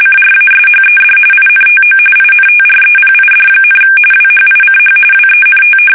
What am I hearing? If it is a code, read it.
RTTY, about 45.45 baud. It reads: CRASHED INTO INTREPID, WHICH WENT
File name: fsk850 / battleshipnj